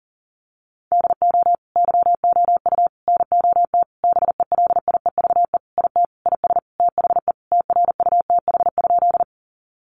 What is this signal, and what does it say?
Morse code, heard as a keyed tone, about 40 wpm, 700 Hz. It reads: DO YOU NOT BELIEVE IT IS THE TRUTH?
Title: Sentences from Top 1000 Words 40wpm